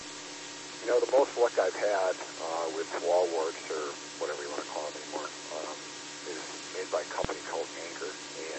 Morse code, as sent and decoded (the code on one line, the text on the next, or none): none